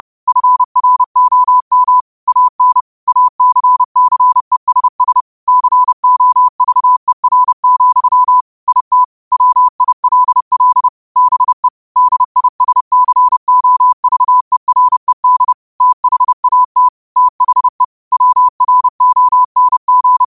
Morse code, heard as a keyed tone, about 30 wpm, 1000 Hz. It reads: FROM AN ACCESS COVER, IT WILL BE DISCOVERED THAT THE WRONG
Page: New quotes every day in morse code at 30 Words per minute.